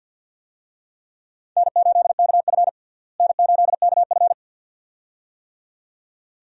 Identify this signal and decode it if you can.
Morse code, heard as a keyed tone, about 50 wpm, 700 Hz. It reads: N8KF N8KF
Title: USA Call Signs - 4 Characters - Repeated 2x - 50wpm